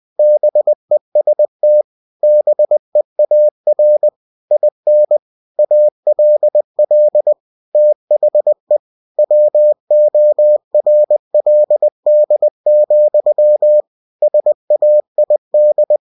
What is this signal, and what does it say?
Morse code, heard as a keyed tone, about 20 wpm, 600 Hz. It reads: BEST BEAR IN ALL THE WORLD, SAID
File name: cw-books / winnie-the-pooh / winnie-20wpm-page-0025